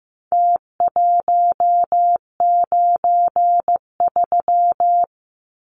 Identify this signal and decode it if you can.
Morse code, heard as a keyed tone, about 15 wpm, 700 Hz. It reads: T193
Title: Numbers - Contest Style - 4-Digit 15wpm